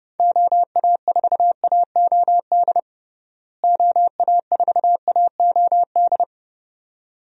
Morse code, heard as a keyed tone, about 32 wpm, 700 Hz. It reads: OA4AOD OA4AOD
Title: USA Call Signs - 6 Characters - Repeated 2x - 30wpm